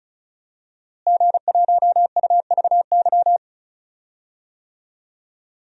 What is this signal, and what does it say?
Morse code, heard as a keyed tone, about 34 wpm, 700 Hz. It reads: G1UVY
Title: DXCC-100 Call Signs - 5 Characters - Repeated 2x - 35wpm